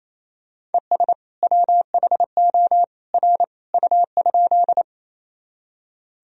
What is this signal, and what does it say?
Morse code, heard as a keyed tone, about 28 wpm, 700 Hz. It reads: ES WHO R U?
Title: Sentences from Top 100 Words and Abbreviations 28wpm